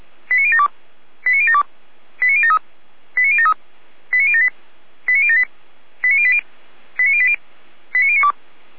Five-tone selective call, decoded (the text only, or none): 89072 89072 89073 89073 89078 89078 89080 89080 89092